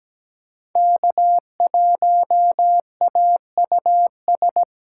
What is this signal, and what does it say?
Morse code, heard as a keyed tone, about 20 wpm, 700 Hz. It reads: K1AUS
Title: Speed Racing - USA 1x3 Call Signs 17wpm